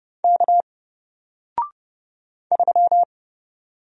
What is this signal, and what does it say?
Morse code, heard as a keyed tone, about 30 wpm, 700 Hz. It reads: K 3
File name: Single-Letter-Number-30wpm